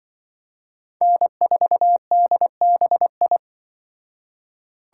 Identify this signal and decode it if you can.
Morse code, heard as a keyed tone, about 23 wpm, 700 Hz. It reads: N4DBI
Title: Speed Racing - USA 1x3 Call Signs 20wpm